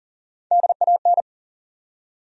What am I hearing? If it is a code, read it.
Morse code, heard as a keyed tone, about 40 wpm, 700 Hz. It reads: DAN
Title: USA Common Names - Max 5 Letters - 40wpm